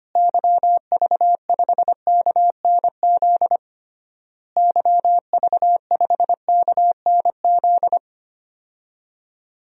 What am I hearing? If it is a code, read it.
Morse code, heard as a keyed tone, about 25 wpm, 700 Hz. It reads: YV5KNZ YV5KNZ
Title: DXCC-100 Call Signs - 6 Characters - Repeated 3x - 25wpm